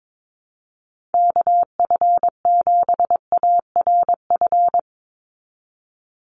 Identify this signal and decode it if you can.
Morse code, heard as a keyed tone, about 22 wpm, 700 Hz. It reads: KF7ARF
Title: USA Call Signs 22wpm